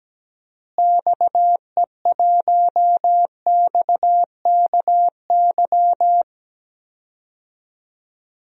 Morse code, heard as a keyed tone, about 17 wpm, 700 Hz. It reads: XE1XKY
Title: DXCC-100 Call Signs - 6 Characters - Repeated 3x - 17wpm